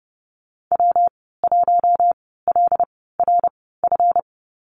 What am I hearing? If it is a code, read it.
Morse code, heard as a keyed tone, about 32 wpm, spaced as for 17 wpm, 700 Hz. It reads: W1LRF
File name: Lesson 37g - Letter x - Call Signs 30wpm